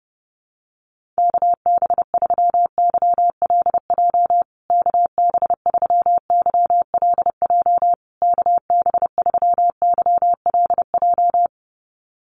Morse code, heard as a keyed tone, about 30 wpm, 700 Hz. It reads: KB3YLJ KB3YLJ KB3YLJ
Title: USA Call Signs Repeated 3x 30wpm